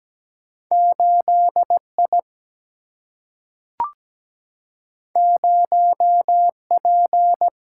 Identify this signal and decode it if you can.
Morse code, heard as a keyed tone, about 17 wpm, 700 Hz. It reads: 8I 0P
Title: Single-Letter Single-Number 17wpm